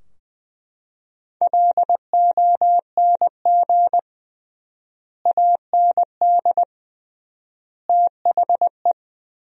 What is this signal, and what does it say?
Morse code, heard as a keyed tone, about 20 wpm, 700 Hz. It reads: LONG AND THE
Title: 3x Word-Spacing - Sentences from Top 400 Words - 20wpm